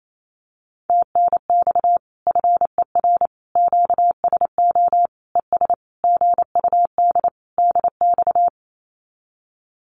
Morse code, heard as a keyed tone, about 28 wpm, 700 Hz. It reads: TNX FER QSO ES GUD DX
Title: Common QSO Phrases Encore 28wpm